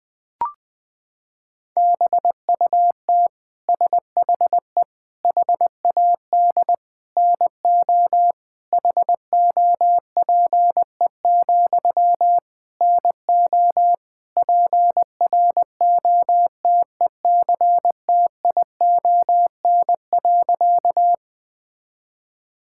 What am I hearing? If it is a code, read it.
Morse code, heard as a keyed tone, about 20 wpm, 700 Hz. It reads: BUT SHE HAD NO HOPE, NO PROTECTION.
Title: Sentences from Top 1600 Words 20wpm